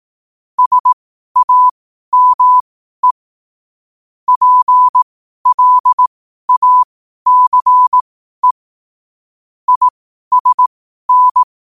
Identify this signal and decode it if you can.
Morse code, heard as a keyed tone, about 18 wpm, spaced as for 12 wpm, 1000 Hz. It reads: SAME PLACE ISN
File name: qotdfiles / 12wpm